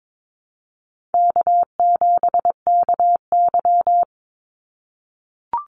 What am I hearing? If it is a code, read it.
Morse code, heard as a keyed tone, about 22 wpm, 700 Hz. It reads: K7KY
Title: Call Signs - 22wpm